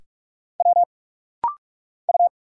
Morse code, heard as a keyed tone, about 46 wpm, 690 Hz. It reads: W U
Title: Single Letters - Rapid-Fire 45wpm